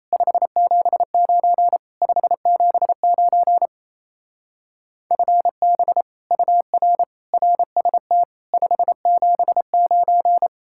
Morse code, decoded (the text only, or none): 579 579 FB UR RST 579